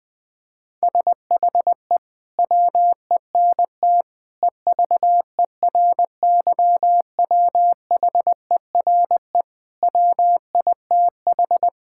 Morse code, heard as a keyed tone, about 21 wpm, 700 Hz. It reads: SHE WENT EVERYWHERE WITH
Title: Sentences from Top 1600 Words 20wpm